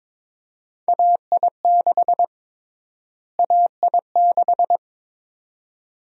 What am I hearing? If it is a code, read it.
Morse code, heard as a keyed tone, about 22 wpm, 700 Hz. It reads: AI6 AI6
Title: USA Call Sign Prefixes - Repeated 2x - 22wpm